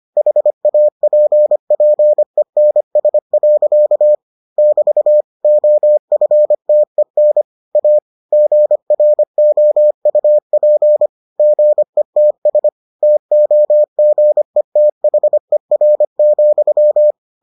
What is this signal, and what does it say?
Morse code, heard as a keyed tone, about 25 wpm, 600 Hz. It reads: HAPPENS. <BT> OFTEN A GROUP GETS TOGETHER,